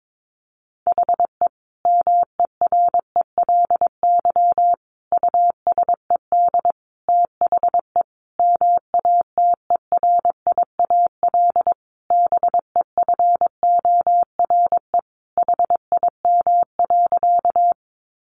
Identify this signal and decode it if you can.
Morse code, heard as a keyed tone, about 22 wpm, 700 Hz. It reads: HE MERELY USED THE MATERIAL BEFORE HIM.
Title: Sentences from Top 1400 Words 22wpm